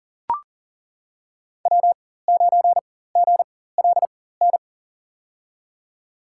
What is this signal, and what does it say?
Morse code, heard as a keyed tone, about 37 wpm, spaced as for 18 wpm, 700 Hz. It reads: W9GLN
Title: USA Call Signs - 5 Characters - 4x Character Spacing - 40wpm